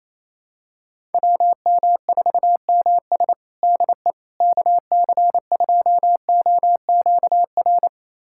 Morse code, heard as a keyed tone, about 27 wpm, 700 Hz. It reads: WM4MS DE KC2OQR
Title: USA Callsign DE Callsign 28wpm